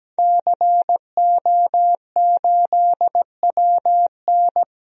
Morse code, heard as a keyed tone, about 17 wpm, 700 Hz. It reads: CO8WN
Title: DXCC-100 Call Signs - 5 Characters - Repeated 2x - 17wpm